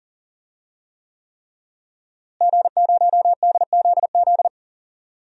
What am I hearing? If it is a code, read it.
Morse code, heard as a keyed tone, about 37 wpm, 700 Hz. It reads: G0DZZ
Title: DXCC-100 Call Signs - 5 Characters - Repeated 3x - 40wpm